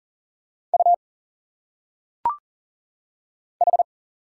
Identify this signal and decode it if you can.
Morse code, heard as a keyed tone, about 40 wpm, 700 Hz. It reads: U H
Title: Single Letters - Phonetic Pronunciation 40wpm